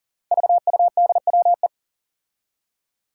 Morse code, heard as a keyed tone, about 40 wpm, 700 Hz. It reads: VUDWE
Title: Five Challenging Letter-Number Combinations 40wpm